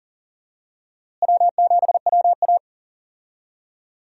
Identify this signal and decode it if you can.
Morse code, heard as a keyed tone, about 40 wpm, 700 Hz. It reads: W7WA
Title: USA Call Signs - 4 Characters - Repeated 3x - 40wpm